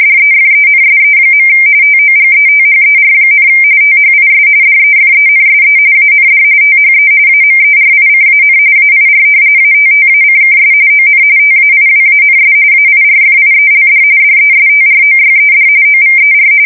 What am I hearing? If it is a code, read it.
RTTY, about 45.45 baud. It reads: DEFENSE. TOGETHER, THIS DEFENSE AND THE AMERICAN AIR ATTACKS ON TAIWANS AIRFIELD DESTROYED OVER
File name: fsk170 / battleshipnj